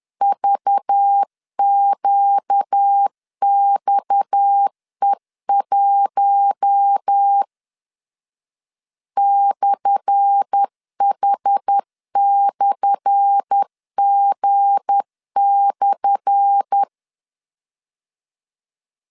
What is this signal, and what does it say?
Morse code, heard as a keyed tone, about 10 wpm, 800 Hz. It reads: VQXE1 /H/G/
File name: LEZIONI / 10 SIGNS / 10A SLASH 10wpm 2808